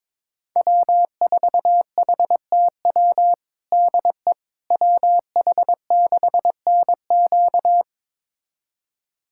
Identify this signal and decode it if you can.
Morse code, heard as a keyed tone, about 22 wpm, 700 Hz. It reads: W4HTW DE WH6NQ
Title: USA Callsign DE Callsign - 5 Characters - 22wpm